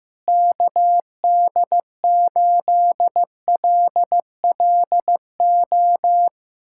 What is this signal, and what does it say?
Morse code, heard as a keyed tone, about 16 wpm, 700 Hz. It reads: KD8LLO
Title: USA Callsign DE Callsign - 6 Characters - 15wpm